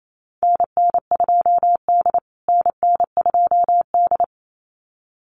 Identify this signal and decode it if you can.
Morse code, heard as a keyed tone, about 28 wpm, 700 Hz. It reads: NN2D NN2D
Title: USA Call Signs Repeated 2x 28wpm